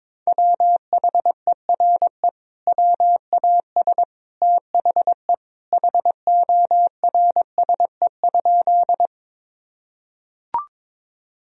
Morse code, heard as a keyed tone, about 22 wpm, 700 Hz. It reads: WHERE WAS THE HORSE?
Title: Sentences from Top 1000 Words - Review Part 1 - 22wpm